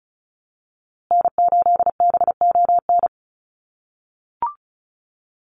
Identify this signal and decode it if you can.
Morse code, heard as a keyed tone, about 34 wpm, 700 Hz. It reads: N8BON
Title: USA Call Signs Repeated 2x 35wpm